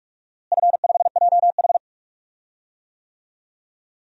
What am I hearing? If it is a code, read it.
Morse code, heard as a keyed tone, about 43 wpm, 700 Hz. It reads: F5JH
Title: DXCC-100 Call Signs - 4 Characters - Repeated 2x - 45wpm